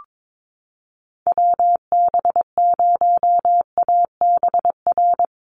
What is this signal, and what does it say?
Morse code, heard as a keyed tone, about 22 wpm, 700 Hz. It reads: WB0ABR
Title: USA Call Signs Repeated 3x 22wpm